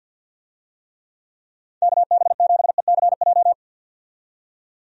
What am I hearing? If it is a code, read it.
Morse code, heard as a keyed tone, about 53 wpm, 700 Hz. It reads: KB7ECJ
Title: USA Call Signs - 6 Characters - 50wpm